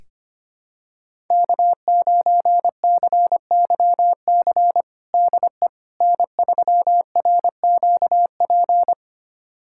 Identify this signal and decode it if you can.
Morse code, heard as a keyed tone, about 25 wpm, 700 Hz. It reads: K9CYC DE N3RQP
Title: USA Callsign DE Callsign - 5 Characters - 25wpm